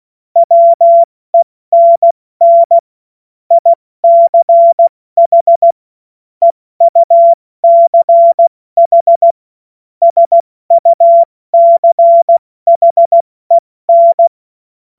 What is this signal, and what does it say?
Morse code, heard as a keyed tone, about 16 wpm, 680 Hz. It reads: WENN ICH EUCH SUCHEN
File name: Morsekurs / Witze / Witz 20. Januar 2026 70 F80